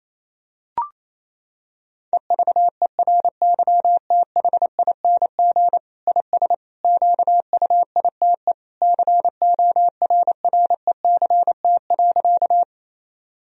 Morse code, heard as a keyed tone, about 28 wpm, 700 Hz. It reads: EVERYTHING IS QUITE CORRECT.
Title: Sentences from Top 1000 Words - Review Part 2 - 28wpm